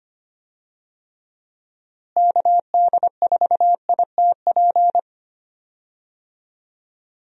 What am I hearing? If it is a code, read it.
Morse code, heard as a keyed tone, about 25 wpm, 700 Hz. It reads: KD4ITP
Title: USA Call Signs - 6 Characters - Repeated 2x - 25wpm